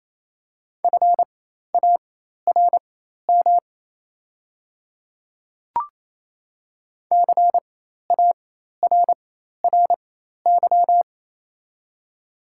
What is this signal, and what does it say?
Morse code, heard as a keyed tone, about 27 wpm, spaced as for 13 wpm, 700 Hz. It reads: FARM CARRY
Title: 4x Character Spacing - Top 500 Words - 28wpm